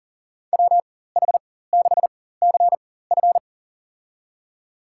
Morse code, heard as a keyed tone, about 39 wpm, spaced as for 19 wpm, 700 Hz. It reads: WH6CF
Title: USA Call Signs - 4x Character Spacing - 40wpm